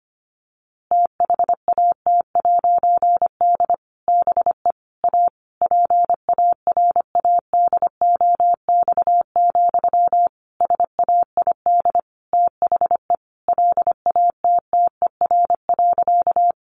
Morse code, heard as a keyed tone, about 25 wpm, 700 Hz. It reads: THAT'D BE A PARADOX, SAID THE LATTER.
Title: A Sound Of Thunder 25wpm